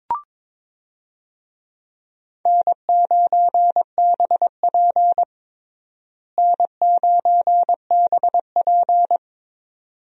Morse code, heard as a keyed tone, about 22 wpm, 700 Hz. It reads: N9BP N9BP
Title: USA Call Signs - 4 Characters - Repeated 2x - 22wpm